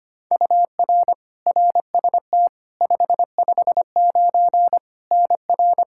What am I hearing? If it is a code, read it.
Morse code, heard as a keyed tone, about 25 wpm, 700 Hz. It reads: UR RST 559 NR
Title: Practice QSO - Exchange 1 - Beginning - 25wpm